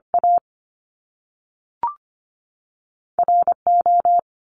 Morse code, heard as a keyed tone, about 25 wpm, 700 Hz. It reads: A R O
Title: Top Two Letter Combinations in Words 25wpm